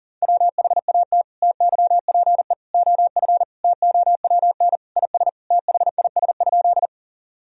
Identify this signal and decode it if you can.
Morse code, heard as a keyed tone, about 40 wpm, 700 Hz. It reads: WHAT TYPE OF TOWN IS THIS?
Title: Sentences from Top 1000 Words - Review Part 3 - 40wpm